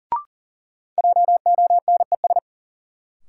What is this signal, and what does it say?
Morse code, heard as a keyed tone, about 40 wpm, 700 Hz. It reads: JONES
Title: Sets of 5 Words - New Word Drill - 40wpm